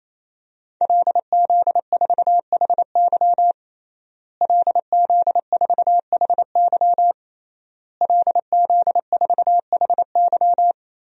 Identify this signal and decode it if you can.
Morse code, heard as a keyed tone, about 28 wpm, 700 Hz. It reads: LZ4HY LZ4HY LZ4HY
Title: DXCC-100 Call Signs - 5 Characters - Repeated 3x - 28wpm